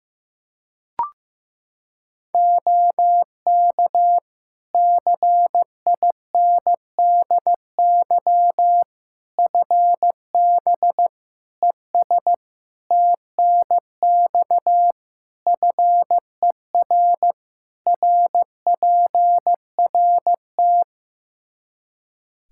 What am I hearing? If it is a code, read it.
Morse code, heard as a keyed tone, about 15 wpm, 700 Hz. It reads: OK CINDY FB ES TNX FER RPRT
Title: K1USN SST QSO Practice 15wpm